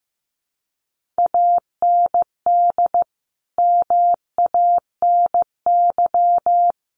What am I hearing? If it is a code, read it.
Morse code, heard as a keyed tone, about 15 wpm, 700 Hz. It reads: AND MANY
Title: Sentences from Top 1500 Words 15wpm